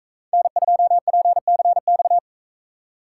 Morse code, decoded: N2PCX

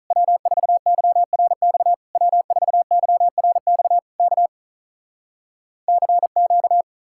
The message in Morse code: W4YRX W4YRX K CQ

